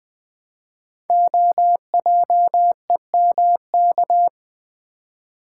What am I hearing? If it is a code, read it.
Morse code, heard as a keyed tone, about 20 wpm, 700 Hz. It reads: OJEMK
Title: Code Groups - 5 Letters - 20wpm